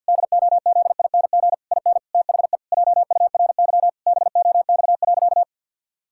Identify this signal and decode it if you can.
Morse code, heard as a keyed tone, about 52 wpm, 700 Hz. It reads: DOZING IN THE JURY BOX.
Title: Pangram Challenge v2 - 50wpm